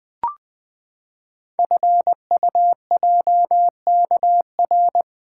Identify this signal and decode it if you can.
Morse code, heard as a keyed tone, about 20 wpm, 700 Hz. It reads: FUJKR
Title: Code Groups - 5 Letters - 20wpm